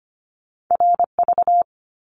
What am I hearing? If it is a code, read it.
Morse code, heard as a keyed tone, about 25 wpm, 700 Hz. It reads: RV
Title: Top Two Letter Combinations in Words 25wpm